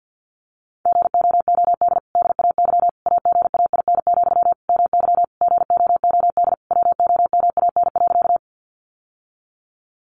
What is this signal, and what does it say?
Morse code, heard as a keyed tone, about 50 wpm, 700 Hz. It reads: GOOD DAY AGAIN, MY GOOD WOMAN.
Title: Sentences from Top 1100 Words 50wpm